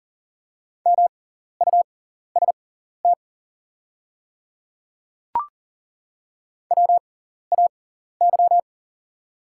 Morse code, heard as a keyed tone, about 37 wpm, spaced as for 14 wpm, 690 Hz. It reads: MUST WAY
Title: 6x Character Spacing - Top 300 Words - 40wpm